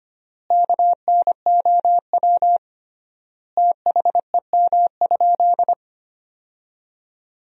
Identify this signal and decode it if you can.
Morse code, heard as a keyed tone, about 25 wpm, 700 Hz. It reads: KNOW THEM?
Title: 3x Word-Spacing - Sentences from Top 100 Words - 25wpm